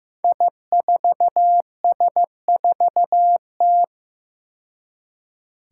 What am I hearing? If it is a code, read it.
Morse code, heard as a keyed tone, about 15 wpm, 700 Hz. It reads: I4S4T
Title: Five Challenging Letter-Number Combinations 15wpm